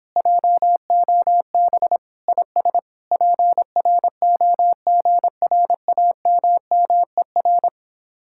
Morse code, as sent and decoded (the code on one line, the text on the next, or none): .--- --- -... / .. ... / .--. .-. --- --. .-. .- -- -- . .-.
JOB IS PROGRAMMER